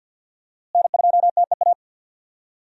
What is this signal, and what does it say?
Morse code, heard as a keyed tone, about 52 wpm, 700 Hz. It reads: N2TEA
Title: USA Call Signs - 5 Characters - Repeated 2x - 50wpm